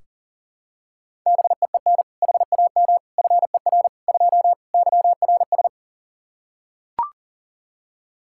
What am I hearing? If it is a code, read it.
Morse code, heard as a keyed tone, about 40 wpm, 700 Hz. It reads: BEEN HAM FER 2 YRS
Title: K1USN SST QSO Practice 40wpm